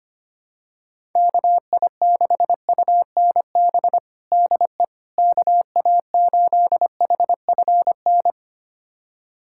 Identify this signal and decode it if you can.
Morse code, heard as a keyed tone, about 25 wpm, 700 Hz. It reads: KI6UNB DE KA8HFN
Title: USA Callsign DE Callsign - 6 Characters - 25wpm